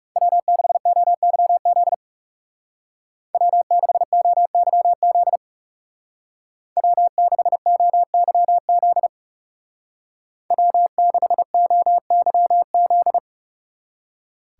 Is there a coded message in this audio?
Yes: Morse code from 0.2 to 13.2 s, heard as a keyed tone, about 39 wpm, 700 Hz. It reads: W6OYZ W6OYZ W6OYZ W6OYZ